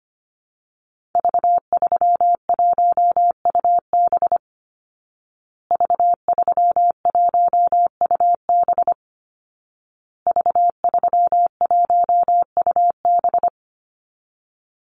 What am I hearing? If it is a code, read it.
Morse code, heard as a keyed tone, about 25 wpm, 700 Hz. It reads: V31UB V31UB V31UB
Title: Call Signs Repeated 2x - 25wpm